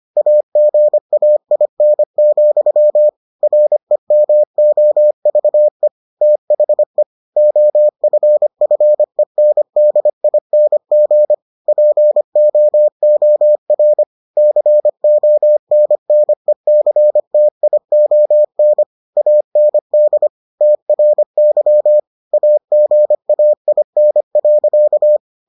Morse code, heard as a keyed tone, about 25 wpm, 600 Hz. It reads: AGAIN, REMOVE THE OFFENDING POOR CONNECTION AND TRY AGAIN.